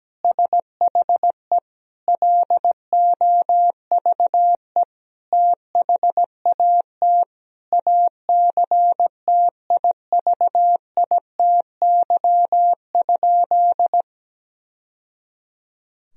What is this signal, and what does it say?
Morse code, heard as a keyed tone, about 17 wpm, 700 Hz. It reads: SHE LOVE THAT ACTIVITY?
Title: Sentences from Top 1000 Words - Review Part 4 - 17wpm